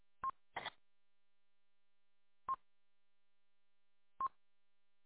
There is no code.